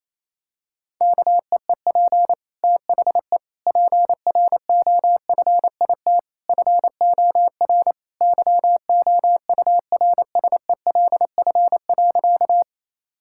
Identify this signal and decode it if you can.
Morse code, heard as a keyed tone, about 28 wpm, 700 Hz. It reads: KEEP THE PROFIT FOR YOURSELF.
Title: Sentences from Top 1000 Words - Review Part 5 - 28wpm